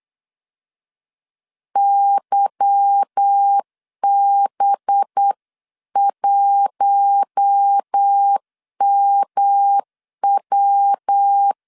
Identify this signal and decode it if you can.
Morse code, heard as a keyed tone, about 8 wpm, 800 Hz. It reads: YB1MW
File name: CW / DET / 1 9 1500C S C 8wpm 5653